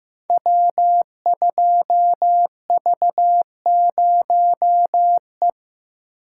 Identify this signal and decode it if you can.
Morse code, heard as a keyed tone, about 15 wpm, 700 Hz. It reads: W2V0E
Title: Code Groups - 5 Letter:Numbers - 15wpm